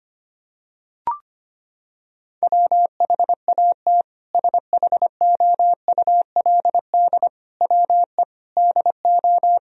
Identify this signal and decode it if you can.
Morse code, heard as a keyed tone, about 25 wpm, 700 Hz. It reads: WHAT SHOULD WE DO
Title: Sentences from Top 1000 Words - Review Part 4 - 25wpm